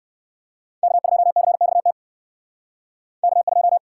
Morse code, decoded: K2CXT K2

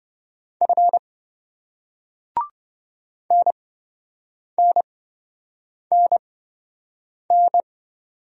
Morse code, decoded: F NNNN

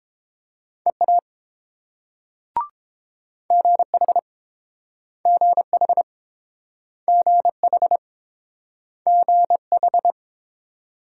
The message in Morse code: EA GH GH GH GH

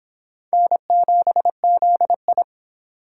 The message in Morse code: N7ZI

